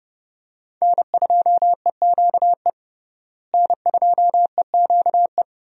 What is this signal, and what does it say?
Morse code, heard as a keyed tone, about 30 wpm, 700 Hz. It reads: N2EQE N2EQE
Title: USA Call Signs - 5 Characters - Repeated 2x - 30wpm